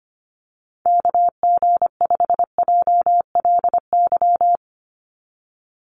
Morse code, heard as a keyed tone, about 25 wpm, 700 Hz. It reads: KG5JLY
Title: USA Call Signs 25wpm